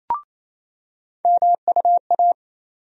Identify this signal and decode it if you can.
Morse code, heard as a keyed tone, about 28 wpm, 700 Hz. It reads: MUA
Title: USA Call Sign Postfixes - 3 Letters 28wpm